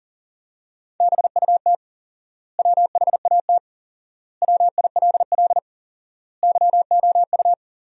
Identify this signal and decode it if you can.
Morse code, heard as a keyed tone, about 37 wpm, 700 Hz. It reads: BUT WHAT WILL YOU
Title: 4x Word-Spacing - Sentences from Top 100 Words - 40wpm